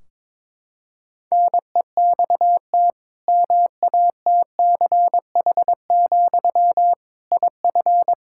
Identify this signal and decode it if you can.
Morse code, heard as a keyed tone, about 22 wpm, 700 Hz. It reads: NEXT MATCH, IF
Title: Sentences from Top 1600 Words 22wpm